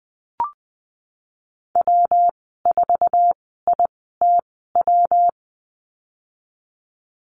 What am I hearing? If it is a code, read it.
Morse code, heard as a keyed tone, about 20 wpm, spaced as for 14 wpm, 700 Hz. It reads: W4ITW
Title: 128 Lesson 21g - Number 6 - Call Signs 20wpm